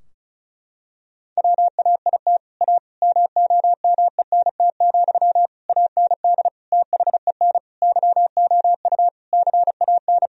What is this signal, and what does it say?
Morse code, heard as a keyed tone, about 34 wpm, 700 Hz. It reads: WAIT A MOMENT, AND THEN YOU CAN
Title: Sentences from Top 1000 Words - Review Part 4 - 35wpm